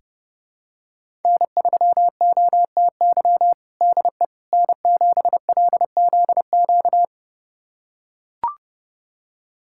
Morse code, heard as a keyed tone, about 30 wpm, 700 Hz. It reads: N3OTY DE N7LZQ
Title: USA Callsign DE Callsign - 5 Characters - 30wpm